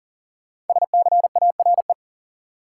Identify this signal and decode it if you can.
Morse code, heard as a keyed tone, about 40 wpm, 700 Hz. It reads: SCARE
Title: Sentences from Top 1000 Words - Review Part 5 - New Word Drill - 40wpm